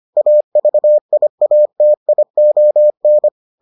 Morse code, heard as a keyed tone, about 25 wpm, 600 Hz. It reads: AVIATION